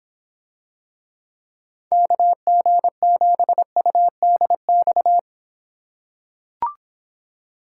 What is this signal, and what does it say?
Morse code, heard as a keyed tone, about 25 wpm, 700 Hz. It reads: KG7UDX